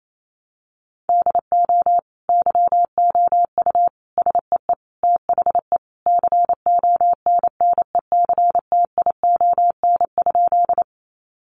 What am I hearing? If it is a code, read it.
Morse code, heard as a keyed tone, about 28 wpm, 700 Hz. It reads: DO YOU SEE THE CONNECTION?
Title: Sentences from Top 1300 Words 28wpm